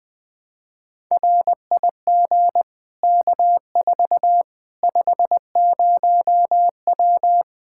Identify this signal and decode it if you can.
Morse code, heard as a keyed tone, about 21 wpm, 700 Hz. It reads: RIG K4 50W
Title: Practice QSO - Exchange 2 - Info - 20wpm